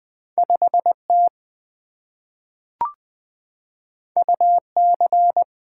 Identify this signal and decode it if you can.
Morse code, heard as a keyed tone, about 20 wpm, 700 Hz. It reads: 5T UC
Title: Two Letter-Number Combinations - 20wpm